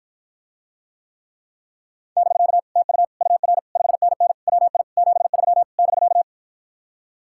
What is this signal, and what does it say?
Morse code, heard as a keyed tone, about 54 wpm, 700 Hz. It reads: <BK> TU UR 5NN WI 73 <BK>